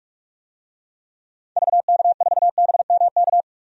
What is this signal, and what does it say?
Morse code, heard as a keyed tone, about 42 wpm, 700 Hz. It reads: VK4BMK
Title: DXCC-100 Call Signs - 6 Characters - 45wpm